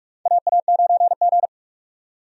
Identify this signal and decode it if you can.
Morse code, heard as a keyed tone, about 47 wpm, 700 Hz. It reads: AA9G
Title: USA Call Signs - 4 Characters - 45wpm